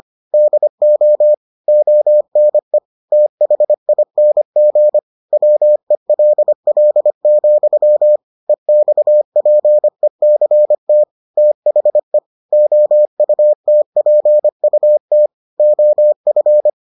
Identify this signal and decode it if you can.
Morse code, heard as a keyed tone, about 25 wpm, 600 Hz. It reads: DO ONE THING WELL, EXPECT THE OUTPUT OF